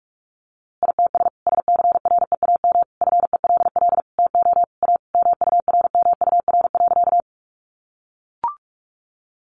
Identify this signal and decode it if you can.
Morse code, heard as a keyed tone, about 44 wpm, 700 Hz. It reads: ITS SCREAM FELL TO A MURMUR.
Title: A Sound Of Thunder 45wpm